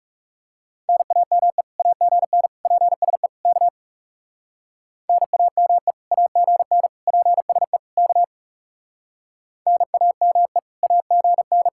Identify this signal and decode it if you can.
Morse code, heard as a keyed tone, about 37 wpm, 700 Hz. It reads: NAME AGN PSE K NAME AGN PSE K NAME AGN
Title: Speed-Racing Common QSO Phrases - 30wpm